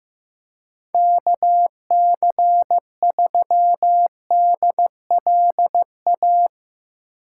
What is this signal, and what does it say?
Morse code, heard as a keyed tone, about 15 wpm, 700 Hz. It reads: KC3DLA
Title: USA Call Signs - 6 Characters - 6x Character Spacing - 15wpm